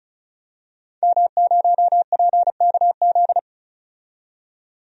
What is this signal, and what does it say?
Morse code, heard as a keyed tone, about 36 wpm, 700 Hz. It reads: M0PKZ
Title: DXCC-100 Call Signs - 5 Characters - 35wpm